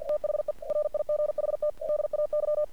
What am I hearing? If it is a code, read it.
Morse code, heard as a keyed tone, about 49 wpm, 600 Hz. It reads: THE RIGHT DAY
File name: archived / CRSnet